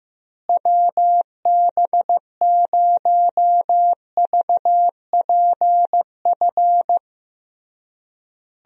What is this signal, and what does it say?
Morse code, heard as a keyed tone, about 16 wpm, 700 Hz. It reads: WB0VPF
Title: USA Call Signs - 6 Characters - Repeated 3x - 15wpm